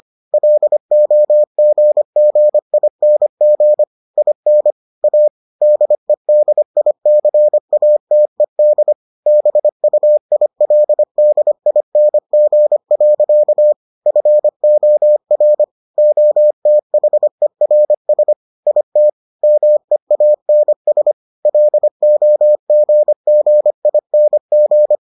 Morse code, heard as a keyed tone, about 25 wpm, 600 Hz. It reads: LOGGING IN A DEDICATED BUILDING. FOR OTHERS IT MEANS LOGGING